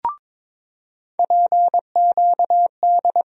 Morse code, heard as a keyed tone, about 22 wpm, 700 Hz. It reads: PQD
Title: USA Call Sign Postfixes - 3 Letters 22wpm